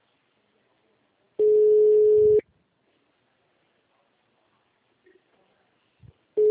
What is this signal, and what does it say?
Ringback tone (Europe, 425 Hz).